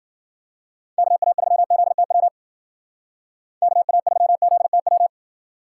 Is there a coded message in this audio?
Yes: Morse code from 1.0 to 5.1 s, heard as a keyed tone, about 52 wpm, 700 Hz. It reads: KA3ZTW KA3ZTW